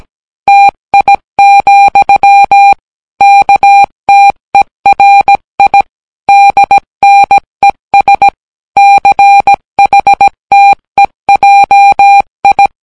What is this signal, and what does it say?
Morse code, heard as a keyed tone, about 17 wpm, 800 Hz. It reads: TI, KTERI DNES CHTEJI